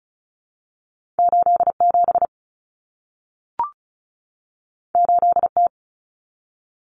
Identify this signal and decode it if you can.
Morse code, heard as a keyed tone, about 36 wpm, 700 Hz. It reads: 87 8T
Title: Common Cut Numbers - 2-Digit 35wpm